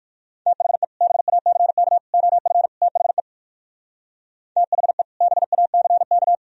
Morse code, heard as a keyed tone, about 58 wpm, 700 Hz. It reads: THE BACK OF THE THE BACK